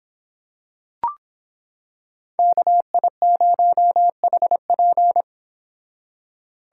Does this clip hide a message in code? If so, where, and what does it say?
Yes: Morse code from 2.4 to 5.2 s, heard as a keyed tone, about 25 wpm, 700 Hz. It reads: KI0HP